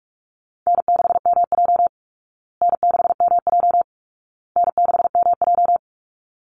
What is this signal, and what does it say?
Morse code, heard as a keyed tone, about 46 wpm, 700 Hz. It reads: N6MJ N6MJ N6MJ
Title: Easy Call Signs Repeated 3x - 45wpm